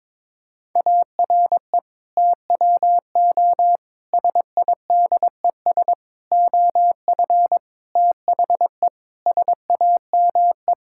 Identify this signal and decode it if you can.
Morse code, heard as a keyed tone, about 22 wpm, 700 Hz. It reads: ARE TWO SIDES OF THE SAME
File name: practice / 22wpm / Sets of 7 Words - 22wpm